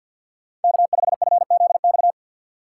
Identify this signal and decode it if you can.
Morse code, heard as a keyed tone, about 59 wpm, 700 Hz. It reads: K5FZX
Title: Speed Racing - USA 1x3 Call Signs 50wpm